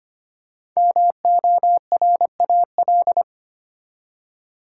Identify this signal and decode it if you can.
Morse code, heard as a keyed tone, about 25 wpm, 700 Hz. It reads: MORAL
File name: 5 Letter Words - First 3 Letters Given - 25wpm